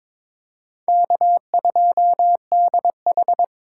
Morse code, heard as a keyed tone, about 26 wpm, 700 Hz. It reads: K2DH
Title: Speed Racing - USA Call Signs 22wpm